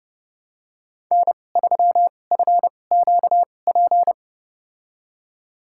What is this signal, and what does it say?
Morse code, heard as a keyed tone, about 32 wpm, spaced as for 22 wpm, 700 Hz. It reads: N3FQP
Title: USA Call Signs - 5 Characters - 2x Character Spacing - 30wpm